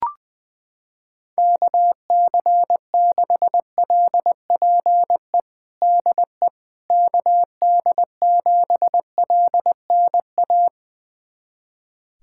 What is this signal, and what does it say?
Morse code, heard as a keyed tone, about 20 wpm, 700 Hz. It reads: KC6LPE DE KD7LNA
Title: USA Callsign DE Callsign - 6 Characters - 20wpm